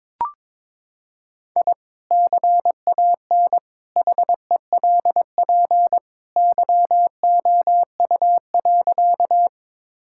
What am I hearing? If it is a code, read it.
Morse code, heard as a keyed tone, about 22 wpm, 700 Hz. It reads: I CAN HELP YOU.
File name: practice / 22wpm / Sentences with 2-Word Phrases - 22wpm